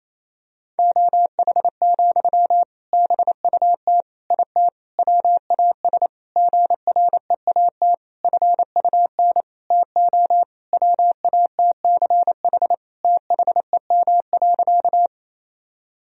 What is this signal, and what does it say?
Morse code, heard as a keyed tone, about 28 wpm, 700 Hz. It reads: OH, BUT IT WAS GREAT FUN TO WATCH THEM.
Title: Sentences from Top 1000 Words - Review Part 4 - 28wpm